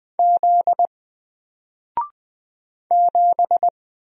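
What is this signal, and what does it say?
Morse code, heard as a keyed tone, about 20 wpm, 700 Hz. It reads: Z 7
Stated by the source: Single letter and number Morse drill at 20 wpm, from N4LCD.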